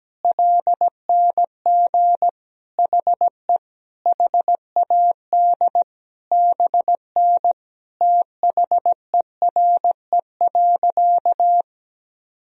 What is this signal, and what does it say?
Morse code, heard as a keyed tone, about 17 wpm, 700 Hz. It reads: LNG HE HAD BN THERE.
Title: Sentences from Top 300 Words and Abbreviations 17wpm